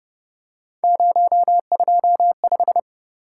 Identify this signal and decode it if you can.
Morse code, heard as a keyed tone, about 29 wpm, 700 Hz. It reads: 025